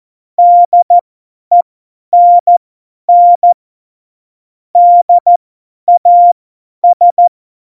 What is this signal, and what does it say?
Morse code, heard as a keyed tone, about 14 wpm, spaced as for 10 wpm, 700 Hz. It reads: DENN DAS
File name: Morsekurs / Witze / Witz 22. Dezember 2020, 50 F70